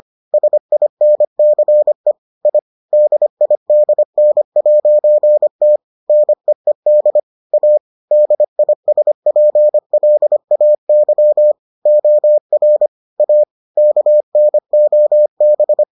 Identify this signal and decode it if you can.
Morse code, heard as a keyed tone, about 25 wpm, 600 Hz. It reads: SINCE I DIDN'T NEED A DISPLAY OR A KNOB